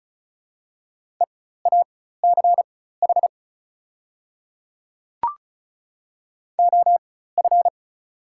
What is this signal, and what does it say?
Morse code, heard as a keyed tone, about 36 wpm, spaced as for 16 wpm, 700 Hz. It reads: EACH OF
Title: 4x Character Spacing - Top 100 Words - 35wpm